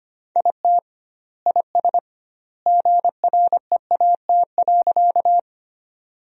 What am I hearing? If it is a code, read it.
Morse code, heard as a keyed tone, about 25 wpm, 700 Hz. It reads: IT IS GREAT.
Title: 2x Word-Spacing - Sentences from Top 200 Words - 25wpm